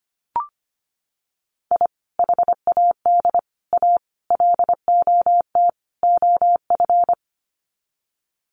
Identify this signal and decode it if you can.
Morse code, heard as a keyed tone, about 25 wpm, 700 Hz. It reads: I HAD A LOT OF
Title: Sets of 5 Words Encore 25wpm